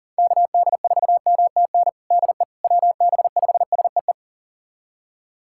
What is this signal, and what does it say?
Morse code, heard as a keyed tone, about 40 wpm, 700 Hz. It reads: KD4MTN DE WB5SEE
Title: USA Callsign DE Callsign - 6 Characters - 40wpm